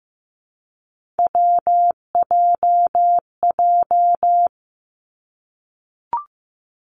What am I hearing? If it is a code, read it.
Morse code, heard as a keyed tone, about 15 wpm, 700 Hz. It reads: WJJ
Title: Three Challenging Letter-Number Combinations 15wpm